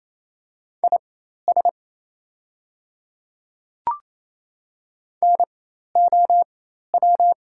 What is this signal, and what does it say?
Morse code, heard as a keyed tone, about 28 wpm, spaced as for 13 wpm, 700 Hz. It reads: IS NOW
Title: 4x Character Spacing - Top 100 Words - 28wpm